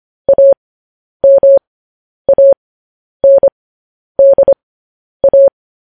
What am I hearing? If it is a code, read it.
Morse code, heard as a keyed tone, about 25 wpm, spaced as for 10 wpm, 550 Hz. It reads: AMANDA
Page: news-headlines-in-morse-code-at-10-wpm